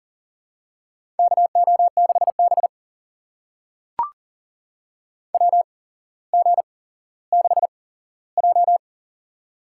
Morse code, heard as a keyed tone, about 37 wpm, spaced as for 17 wpm, 700 Hz. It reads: KO6B WG6J
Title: USA Call Signs - 4 Characters - 8x Character Spacing - 40wpm